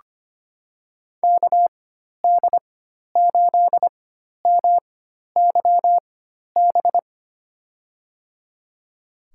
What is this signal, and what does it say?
Morse code, heard as a keyed tone, about 25 wpm, spaced as for 12 wpm, 700 Hz. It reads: KD8MYB
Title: USA Call Signs - 6 Characters - 4x Character Spacing - 25wpm